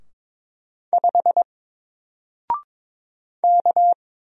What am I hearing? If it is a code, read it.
Morse code, heard as a keyed tone, about 22 wpm, 700 Hz. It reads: 5 K
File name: Single-Letter-Number-22wpm